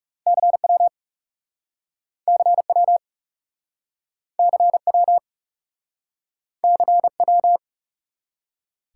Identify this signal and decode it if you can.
Morse code, heard as a keyed tone, about 38 wpm, 700 Hz. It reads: CW CW CW CW